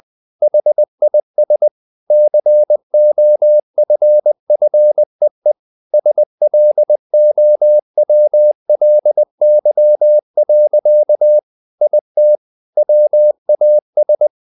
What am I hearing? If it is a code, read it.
Morse code, heard as a keyed tone, about 20 wpm, 600 Hz. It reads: HIS COFFEE SLOWLY. IT WAS
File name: the old man and the sea / the old man and the sea part 1 20wpm